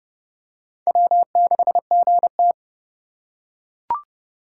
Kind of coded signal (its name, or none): Morse code